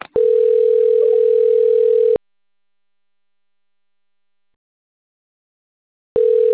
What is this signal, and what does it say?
Ringback tone (North America), 440 + 480 Hz.
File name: rg-1600-unknown- - - .94260